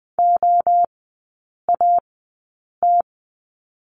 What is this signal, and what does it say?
Morse code, heard as a keyed tone, about 20 wpm, spaced as for 8 wpm, 690 Hz. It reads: OAT
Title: 198 Lesson 32e - Letter k - Three Characters 20wpm